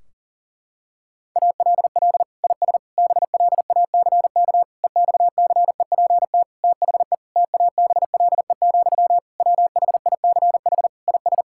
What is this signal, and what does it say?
Morse code, heard as a keyed tone, about 40 wpm, 700 Hz. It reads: ALL IS BLACK EXCEPT THE TABLE, WHICH IS
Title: Sentences from Top 1600 Words 40wpm